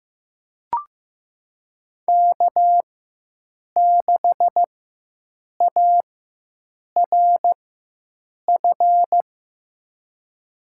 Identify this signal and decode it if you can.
Morse code, heard as a keyed tone, about 15 wpm, spaced as for 7 wpm, 700 Hz. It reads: K6ARF
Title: USA Call Signs - 5 Characters - 4x Character Spacing - 15wpm